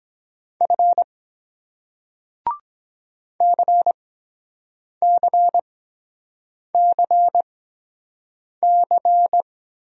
Morse code, heard as a keyed tone, about 23 wpm, spaced as for 7 wpm, 700 Hz. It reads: F CCCC